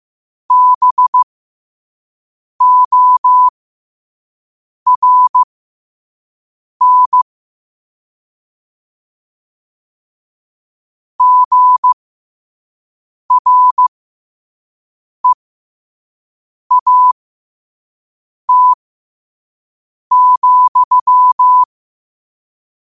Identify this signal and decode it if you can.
Morse code, heard as a keyed tone, about 15 wpm, spaced as for 5 wpm, 1000 Hz. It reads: BORN GREAT,
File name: qotdfiles / 5wpm